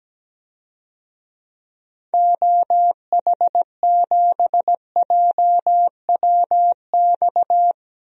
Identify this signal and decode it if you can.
Morse code, heard as a keyed tone, about 17 wpm, 700 Hz. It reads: OH7JWX
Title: DXCC-100 Call Signs - 6 Characters - 17wpm